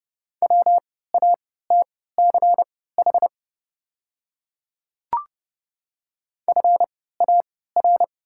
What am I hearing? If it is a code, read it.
Morse code, heard as a keyed tone, about 30 wpm, spaced as for 17 wpm, 700 Hz. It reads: WATCH FAR
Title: 3x Character Spacing - Top 300 Words - 30wpm